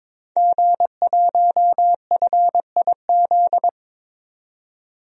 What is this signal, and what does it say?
Morse code, heard as a keyed tone, about 21 wpm, 700 Hz. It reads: G1FIZ
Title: DXCC-100 Call Signs - 5 Characters - 22wpm